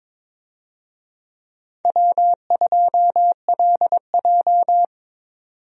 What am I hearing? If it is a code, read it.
Morse code, heard as a keyed tone, about 22 wpm, 700 Hz. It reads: W2LJ
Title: USA Call Signs - 4 Characters - 22wpm